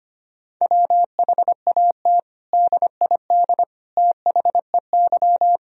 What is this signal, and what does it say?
Morse code, heard as a keyed tone, about 25 wpm, 700 Hz. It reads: WHAT DID THEY
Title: Sentences from Top 1000 Words - Review Part 4 - 25wpm